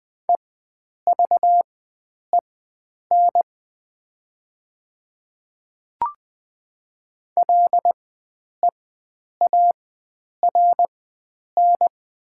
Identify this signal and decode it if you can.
Morse code, heard as a keyed tone, about 20 wpm, spaced as for 9 wpm, 700 Hz. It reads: EVEN LEARN
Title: 4x Character Spacing - Top 300 Words - 20wpm